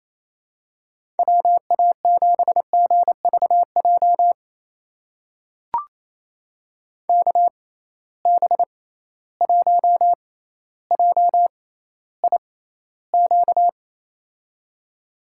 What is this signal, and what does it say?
Morse code, heard as a keyed tone, about 28 wpm, spaced as for 14 wpm, 700 Hz. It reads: WA7GVJ KB1JIQ
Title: USA Call Signs - 6 Characters - 6x Character Spacing - 28wpm